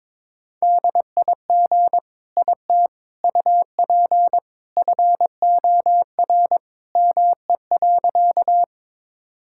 Morse code, heard as a keyed tone, about 22 wpm, 700 Hz. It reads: DIG IT UP FOR ME.
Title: Sentences from Top 1600 Words 22wpm